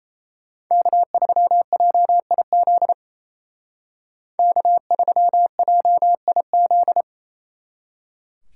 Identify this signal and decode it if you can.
Morse code, heard as a keyed tone, about 31 wpm, 700 Hz. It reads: K3JIZ K3JIZ
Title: Speed Racing - USA Call Signs 28wpm